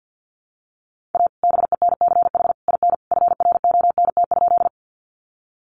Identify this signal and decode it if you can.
Morse code, heard as a keyed tone, about 50 wpm, 700 Hz. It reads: A BENCH IN FRONT?
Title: Sentences from Top 1400 Words 50wpm